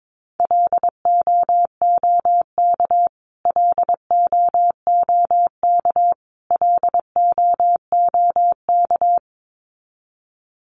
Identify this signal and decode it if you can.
Morse code, heard as a keyed tone, about 22 wpm, 700 Hz. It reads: LOOK LOOK LOOK
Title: Top 500 Words Repeated 22wpm